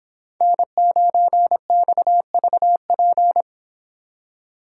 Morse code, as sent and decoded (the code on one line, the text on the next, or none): -. ----. -..- ...- .--.
N9XVP